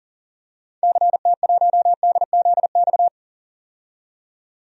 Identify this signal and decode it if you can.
Morse code, heard as a keyed tone, about 40 wpm, 700 Hz. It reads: CT1DZX
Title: DXCC-100 Call Signs - 6 Characters - 40wpm